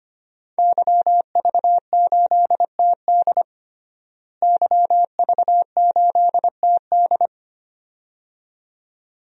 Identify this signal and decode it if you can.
Morse code, heard as a keyed tone, about 25 wpm, 700 Hz. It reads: YV8TD YV8TD
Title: DXCC-100 Call Signs - 5 Characters - Repeated 2x - 25wpm